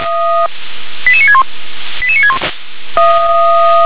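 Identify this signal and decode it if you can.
Five-tone selective call (ZVEI), heard as quick runs of five tones, tones of about 70 ms. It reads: 88961 88961
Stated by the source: ZVEI five-tone selective call received on an SDR.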